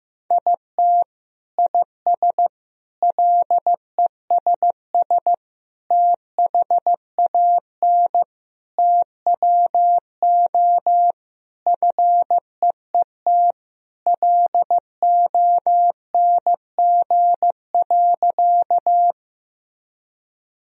Morse code, heard as a keyed tone, about 15 wpm, 700 Hz. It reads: IT IS LESS THAN TWO FEET LONG.
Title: Sentences from Top 1000 Words - Review Part 2 - 15wpm